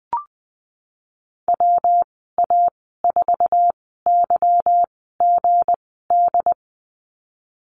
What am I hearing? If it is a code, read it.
Morse code, heard as a keyed tone, about 20 wpm, spaced as for 15 wpm, 700 Hz. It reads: WA4YGD
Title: 167 Lesson 27g - Number 7 - Call Signs 20wpm